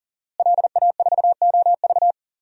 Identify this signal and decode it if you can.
Morse code, heard as a keyed tone, about 40 wpm, 700 Hz. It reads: LA4OV
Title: DXCC-100 Call Signs - 5 Characters - Repeated 3x - 40wpm